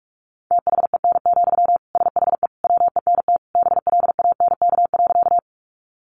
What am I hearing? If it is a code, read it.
Morse code, heard as a keyed tone, about 44 wpm, 700 Hz. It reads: THEN, SHE WENT BLANK.
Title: Sentences from Top 1500 Words 45wpm